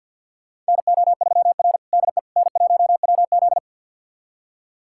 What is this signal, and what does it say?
Morse code, heard as a keyed tone, about 47 wpm, 700 Hz. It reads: NO3R DE N1WZ
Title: USA Callsign DE Callsign - 4 Characters - 50wpm